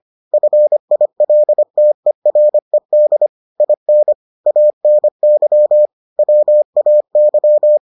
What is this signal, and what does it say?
Morse code, heard as a keyed tone, about 25 wpm, 600 Hz. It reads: FILTERED IN ANY WAY